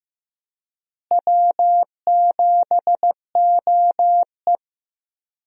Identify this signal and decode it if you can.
Morse code, heard as a keyed tone, about 15 wpm, 700 Hz. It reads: W7OE
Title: USA Call Signs - 4 Characters - Repeated 3x - 15wpm